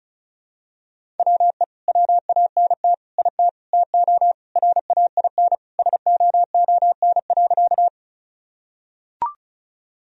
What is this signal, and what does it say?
Morse code, heard as a keyed tone, about 34 wpm, 700 Hz. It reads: WE WANT IT TO RAIN SOON.
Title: Sentences from Top 1000 Words - Review Part 1 - 35wpm